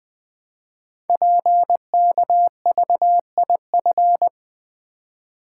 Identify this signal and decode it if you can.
Morse code, heard as a keyed tone, about 20 wpm, 700 Hz. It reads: PKVIF
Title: Code Groups - 5 Letters - 20wpm